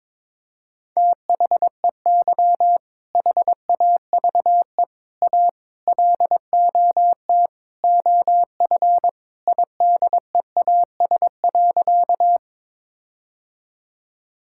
Morse code, heard as a keyed tone, about 22 wpm, 700 Hz. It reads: THEY HAVE A LOT OF IDEAS.
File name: practice / 22wpm / Sentences with 4-Word Phrases - 22wpm